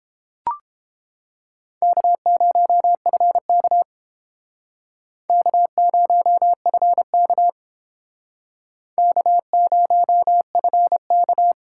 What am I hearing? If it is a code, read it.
Morse code, heard as a keyed tone, about 31 wpm, 700 Hz. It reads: K0FK K0FK K0FK